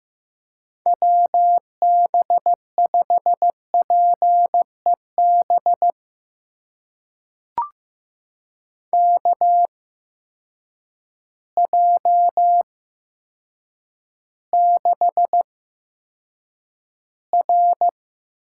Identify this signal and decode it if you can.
Morse code, heard as a keyed tone, about 15 wpm, spaced as for 8 wpm, 700 Hz. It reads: WB5PEB KJ6R
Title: USA Call Signs - 8x Character Spacing - 15wpm